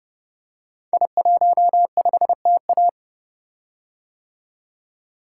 Morse code, heard as a keyed tone, about 30 wpm, 700 Hz. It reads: I15TA
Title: Code Groups - 5 Letter:Numbers - 30wpm